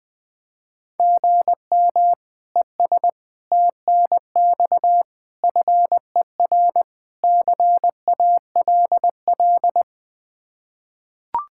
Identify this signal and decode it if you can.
Morse code, heard as a keyed tone, about 20 wpm, 700 Hz. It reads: GM ES TNX FER CALL
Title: Practice QSO - Exchange 1 - Beginning - 20wpm